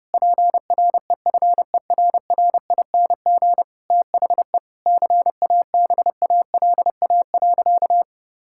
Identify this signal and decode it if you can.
Morse code, heard as a keyed tone, about 30 wpm, 700 Hz. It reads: PREFERRING THE CABALA.
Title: Pangram Challenge v2 - 30wpm